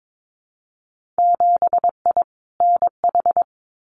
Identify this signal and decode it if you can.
Morse code, heard as a keyed tone, about 22 wpm, 700 Hz. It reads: 7I NH
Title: Winter Field Day Exchanges Encore 22wpm